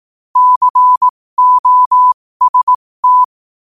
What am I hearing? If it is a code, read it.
Morse code, heard as a keyed tone, about 18 wpm, spaced as for 15 wpm, 1000 Hz. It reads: COST
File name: qotdfiles / 15wpm